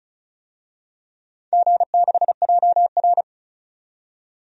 Morse code, heard as a keyed tone, about 36 wpm, 700 Hz. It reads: G6JR
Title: DXCC-100 Call Signs - 4 Characters - 35wpm